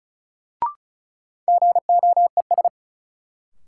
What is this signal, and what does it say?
Morse code, heard as a keyed tone, about 36 wpm, 700 Hz. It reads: GOES
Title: Sentences from Top 1000 Words - Review Part 5 - New Word Drill - 35wpm